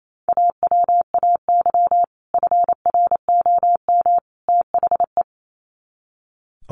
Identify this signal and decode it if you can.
Morse code, heard as a keyed tone, about 28 wpm, 700 Hz. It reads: AWAY FROM THE
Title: Sets of 3 Words 28wpm